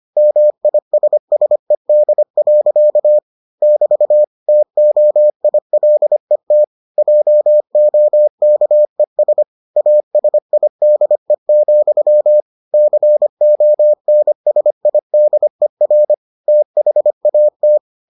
Morse code, heard as a keyed tone, about 25 wpm, 600 Hz. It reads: MISSED. <BT> TOILET JOKES ASIDE, CONSIDER THAT